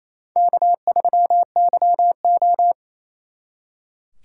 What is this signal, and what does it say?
Morse code, heard as a keyed tone, about 34 wpm, 700 Hz. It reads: K3YO
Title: Speed Racing - USA 1x2 Call Signs 28wpm